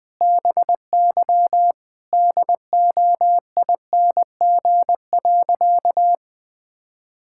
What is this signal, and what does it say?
Morse code, heard as a keyed tone, about 20 wpm, 700 Hz. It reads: BY DOING.
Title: Sentences with 4-Word Phrases - Spoken Beginning - 20wpm